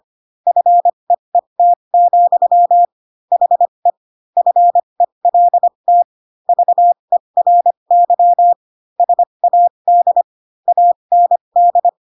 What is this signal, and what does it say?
Morse code, heard as a keyed tone, about 25 wpm, spaced as for 22 wpm, 700 Hz. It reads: FEET, HE FELT VERY SAD AND
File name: Pinocchio / Pinocchio ch7 18 25 ik1hge